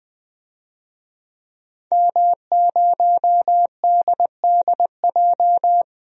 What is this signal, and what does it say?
Morse code, heard as a keyed tone, about 20 wpm, 700 Hz. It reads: M0DDJ
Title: DXCC-100 Call Signs - 5 Characters - 20wpm